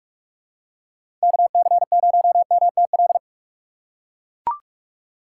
Morse code, heard as a keyed tone, about 43 wpm, 700 Hz. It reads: KC0MTL
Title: USA Call Signs - 6 Characters - 3x Character Spacing - 45wpm